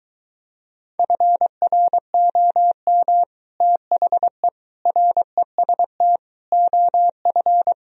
Morse code, heard as a keyed tone, about 23 wpm, 700 Hz. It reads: FROM THE REST OF